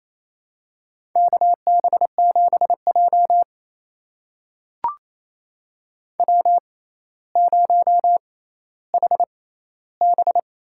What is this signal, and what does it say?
Morse code, heard as a keyed tone, about 28 wpm, spaced as for 14 wpm, 700 Hz. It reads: KB7J W0HB
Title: USA Call Signs - 4 Characters - 6x Character Spacing - 28wpm